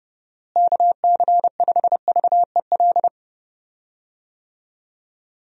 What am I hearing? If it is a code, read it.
Morse code, heard as a keyed tone, about 30 wpm, 700 Hz. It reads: KC5VEL
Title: USA Call Signs - 6 Characters - Repeated 3x - 30wpm